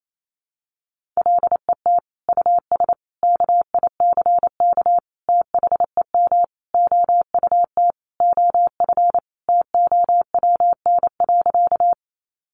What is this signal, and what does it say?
Morse code, heard as a keyed tone, about 28 wpm, 700 Hz. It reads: LET US KICK THEM OUT OF TOWN.
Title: Sentences from Top 1300 Words 28wpm